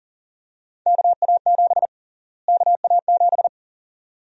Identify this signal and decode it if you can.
Morse code, heard as a keyed tone, about 40 wpm, 700 Hz. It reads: KA7 KA7
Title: USA Call Sign Prefixes - Repeated 2x - 40wpm